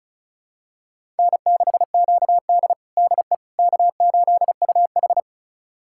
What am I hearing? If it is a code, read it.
Morse code, heard as a keyed tone, about 34 wpm, 700 Hz. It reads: N6QD DE K8UH
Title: USA Callsign DE Callsign - 4 Characters - 35wpm